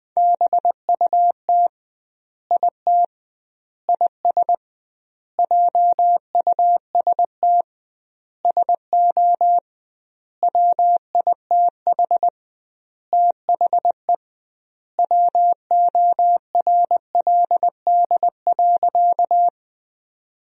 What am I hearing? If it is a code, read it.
Morse code, heard as a keyed tone, about 20 wpm, 700 Hz. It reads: BUT IT IS JUST SO WITH THE WORLD.
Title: 2x Word-Spacing - Sentences from Top 200 Words - 20wpm